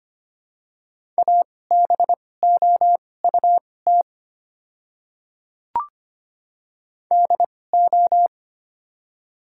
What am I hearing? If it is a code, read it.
Morse code, heard as a keyed tone, about 25 wpm, spaced as for 18 wpm, 700 Hz. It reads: ABOUT DO
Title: 2x Character Spacing - Top 100 Words - 25wpm